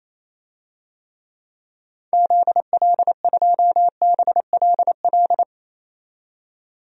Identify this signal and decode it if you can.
Morse code, heard as a keyed tone, about 28 wpm, 700 Hz. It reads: ZL2BLL
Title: DXCC-100 Call Signs - 6 Characters - 28wpm